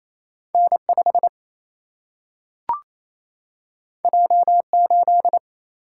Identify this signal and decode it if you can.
Morse code, heard as a keyed tone, about 28 wpm, 700 Hz. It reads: N5 J8
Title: Single-Letter Single-Number 28wpm